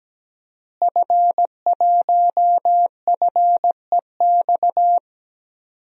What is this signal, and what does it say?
Morse code, heard as a keyed tone, about 17 wpm, 700 Hz. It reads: F1FEX
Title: DXCC-100 Call Signs - 5 Characters - Repeated 2x - 17wpm